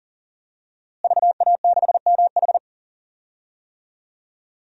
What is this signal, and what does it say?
Morse code, heard as a keyed tone, about 37 wpm, 700 Hz. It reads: VA6MH
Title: DXCC-100 Call Signs - 5 Characters - Repeated 3x - 40wpm